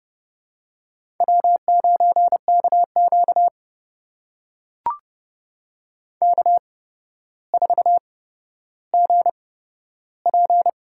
Morse code, heard as a keyed tone, about 30 wpm, spaced as for 13 wpm, 700 Hz. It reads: W9KQ K4GP
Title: USA Call Signs - 4 Characters - 8x Character Spacing - 30wpm